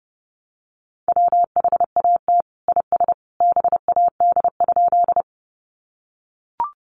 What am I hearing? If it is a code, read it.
Morse code, heard as a keyed tone, about 30 wpm, 700 Hz. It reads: WHAT IS BAD?
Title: Sentences from Top 700 Words 30wpm